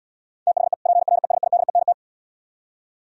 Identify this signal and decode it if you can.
Morse code, heard as a keyed tone, about 75 wpm, 700 Hz. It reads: THE PRESIDENT